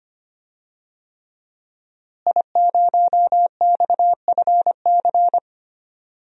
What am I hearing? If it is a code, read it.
Morse code, heard as a keyed tone, about 25 wpm, 700 Hz. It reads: I0XFC
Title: DXCC-100 Call Signs - 5 Characters - Repeated 3x - 25wpm